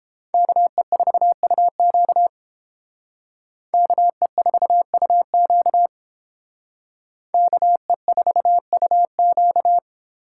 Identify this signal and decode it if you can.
Morse code, heard as a keyed tone, about 29 wpm, 700 Hz. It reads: KE4UQ KE4UQ KE4UQ